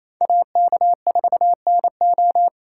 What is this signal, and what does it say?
Morse code, heard as a keyed tone, about 35 wpm, 700 Hz. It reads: AK4NO
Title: Speed Racing - USA 2x2 Call Signs 28wpm